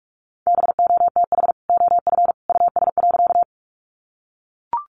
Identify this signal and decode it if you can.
Morse code, heard as a keyed tone, about 44 wpm, 700 Hz. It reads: BOTH OF US.
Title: Sentences from Top 300 Words 45wpm